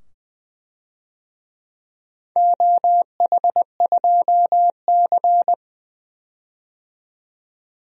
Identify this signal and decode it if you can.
Morse code, heard as a keyed tone, about 20 wpm, 700 Hz. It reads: OH2C
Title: DXCC-100 Call Signs - 4 Characters - 20wpm